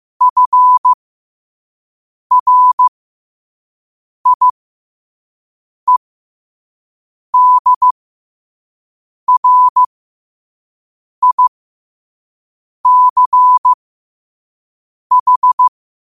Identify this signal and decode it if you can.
Morse code, heard as a keyed tone, about 15 wpm, spaced as for 5 wpm, 1000 Hz. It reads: FRIEDRICH
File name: qotdfiles / 5wpm